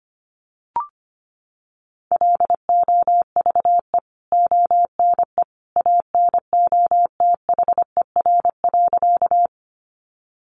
Morse code, heard as a keyed tone, about 25 wpm, 700 Hz. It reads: LOVE ONE ANOTHER.
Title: Sentences from Top 600 Words 25wpm